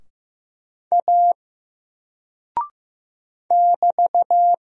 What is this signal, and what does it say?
Morse code, heard as a keyed tone, about 15 wpm, 700 Hz. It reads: A <BT>
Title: Single Character (Letter:Number:Punctuation:Prosign) 15wpm